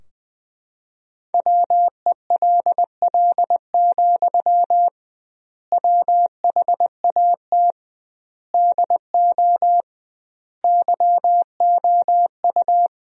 Morse code, heard as a keyed tone, about 20 wpm, 700 Hz. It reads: WELL, WHAT DO YOU
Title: 2x Word-Spacing - Sentences from Top 200 Words - 20wpm